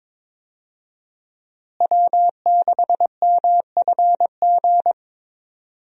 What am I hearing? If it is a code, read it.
Morse code, heard as a keyed tone, about 22 wpm, 700 Hz. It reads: W6MFG
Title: USA Call Signs - 5 Characters - Repeated 3x - 22wpm